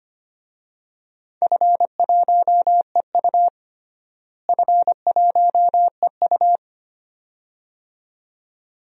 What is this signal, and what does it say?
Morse code, heard as a keyed tone, about 25 wpm, 700 Hz. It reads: F1EU F1EU
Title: DXCC-100 Call Signs - 4 Characters - Repeated 2x - 25wpm